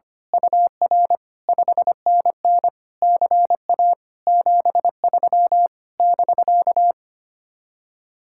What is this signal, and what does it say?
Morse code, heard as a keyed tone, about 24 wpm, 700 Hz. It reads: UR 5NN CA 73 <BK>
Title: Speed-Racing Common QSO Phrases - 25wpm